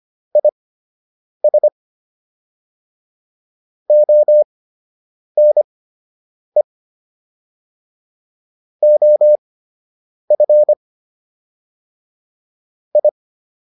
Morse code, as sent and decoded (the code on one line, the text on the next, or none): .. ... / --- -. / --- ..-. / ..
IS ONE OF I